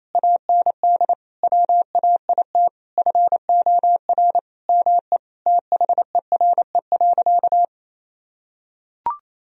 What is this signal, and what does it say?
Morse code, heard as a keyed tone, about 28 wpm, 700 Hz. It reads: AND WAIT FOR ME THERE.
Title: Sentences from Top 1000 Words - Review Part 4 - 28wpm